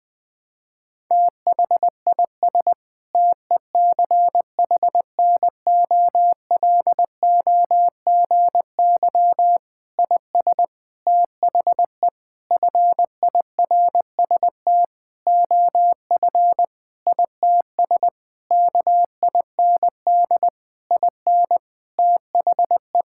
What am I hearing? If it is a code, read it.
Morse code, heard as a keyed tone, about 21 wpm, 700 Hz. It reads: THIS TECHNOLOGY IS THE FIRST OF ITS KIND IN THE
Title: Sentences with 8-Word Phrases - 20wpm